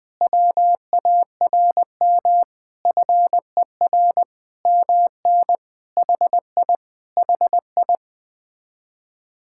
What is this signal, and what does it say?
Morse code, heard as a keyed tone, about 20 wpm, 700 Hz. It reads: WARM FER MN HI HI
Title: Speed-Racing Common QSO Phrases Encore - 17wpm